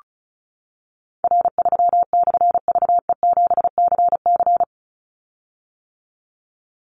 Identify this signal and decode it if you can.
Morse code, heard as a keyed tone, about 34 wpm, 700 Hz. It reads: R3/VE7CC
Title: Call Signs - 35wpm